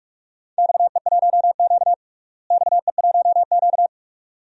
Morse code, heard as a keyed tone, about 46 wpm, 700 Hz. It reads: XE1Q XE1Q
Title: DXCC-100 Call Signs - 4 Characters - Repeated 2x - 45wpm